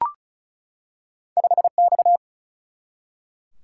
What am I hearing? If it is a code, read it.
Morse code, heard as a keyed tone, about 36 wpm, 700 Hz. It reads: 5X
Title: Single-Letter Single-Number 35wpm